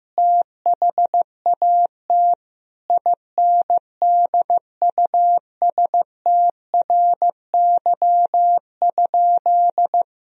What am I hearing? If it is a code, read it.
Morse code, heard as a keyed tone, about 15 wpm, 700 Hz. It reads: THAT INDUSTRY?
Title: Sentences from Top 1000 Words - Review Part 4 - 15wpm